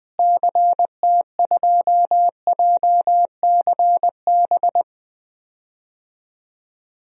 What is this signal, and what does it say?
Morse code, heard as a keyed tone, about 21 wpm, 700 Hz. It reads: CT2JCB
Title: DXCC-100 Call Signs - 6 Characters - Repeated 2x - 20wpm